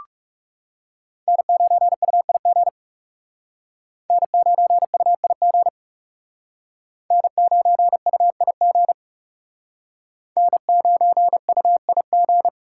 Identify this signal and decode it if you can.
Morse code, heard as a keyed tone, about 36 wpm, 700 Hz. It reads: N9UIG N9UIG N9UIG N9UIG
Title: Speed Racing - USA 1x3 Call Signs 30wpm